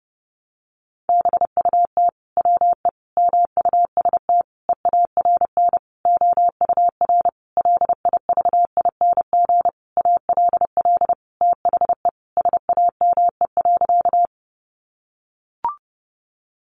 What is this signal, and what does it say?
Morse code, heard as a keyed tone, about 30 wpm, 700 Hz. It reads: BUT WE MUST EARN OUR LIVING ALL THE SAME.
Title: Sentences from Top 1500 Words 30wpm